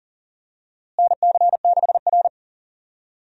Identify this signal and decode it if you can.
Morse code, heard as a keyed tone, about 37 wpm, 700 Hz. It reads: NC6R
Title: USA Call Signs - 4 Characters - Repeated 3x - 40wpm